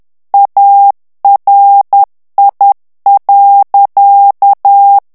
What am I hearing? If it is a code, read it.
Morse code, heard as a keyed tone, about 10 wpm, 800 Hz. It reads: ARI.
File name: ITALIEN texte clair 2 vitesse 10 mots minute